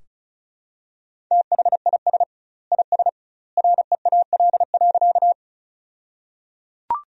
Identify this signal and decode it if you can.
Morse code, heard as a keyed tone, about 34 wpm, 700 Hz. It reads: THIS IS REAL.
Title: 2x Word-Spacing - Sentences from Top 300 Words - 35wpm